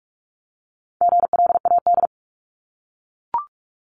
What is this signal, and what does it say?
Morse code, heard as a keyed tone, about 44 wpm, 700 Hz. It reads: GLAD
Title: Top 751-1000 Words 45wpm